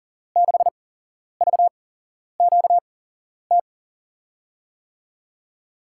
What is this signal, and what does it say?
Morse code, heard as a keyed tone, about 42 wpm, spaced as for 11 wpm, 700 Hz. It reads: 6VQT
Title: USA Call Signs - 6 Characters - 8x Character Spacing - 40wpm